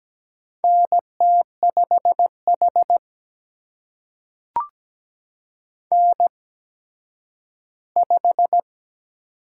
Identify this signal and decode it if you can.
Morse code, heard as a keyed tone, about 17 wpm, spaced as for 10 wpm, 700 Hz. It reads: NT5H N5
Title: USA Call Signs - 4 Characters - 8x Character Spacing - 17wpm